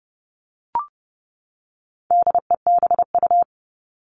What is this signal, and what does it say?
Morse code, heard as a keyed tone, about 30 wpm, 700 Hz. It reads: DEBU
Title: All Cut Numbers - 4-Digit 30wpm